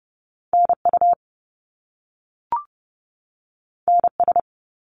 Morse code, heard as a keyed tone, about 30 wpm, 700 Hz. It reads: NU NS
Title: Canadian Provinces and Territories 30wpm